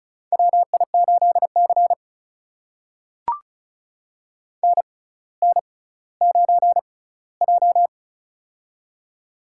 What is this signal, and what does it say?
Morse code, heard as a keyed tone, about 36 wpm, spaced as for 18 wpm, 700 Hz. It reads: WI8C NN9J
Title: USA Call Signs - 4 Characters - 6x Character Spacing - 35wpm